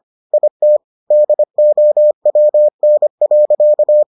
Morse code, heard as a keyed tone, about 25 wpm, 600 Hz. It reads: IT DOWN.